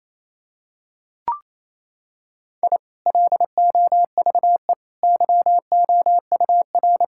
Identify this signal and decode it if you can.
Morse code, heard as a keyed tone, about 28 wpm, 700 Hz. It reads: I LOVE YOUR
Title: Sentences from Top 1000 Words - Review Part 6 - 28wpm